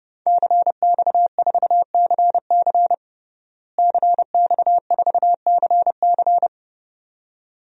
Morse code, heard as a keyed tone, about 30 wpm, 700 Hz. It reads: CX4CC CX4CC
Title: DXCC-100 Call Signs - 5 Characters - Repeated 2x - 30wpm